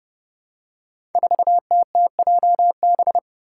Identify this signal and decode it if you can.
Morse code, heard as a keyed tone, about 30 wpm, 700 Hz. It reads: 4TTJB
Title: Five Challenging Letter-Number Combinations 30wpm